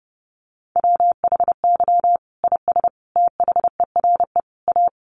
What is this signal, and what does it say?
Morse code, heard as a keyed tone, about 30 wpm, 700 Hz. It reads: WHY IS THERE A
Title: Sentences from Top 1200 Words 30wpm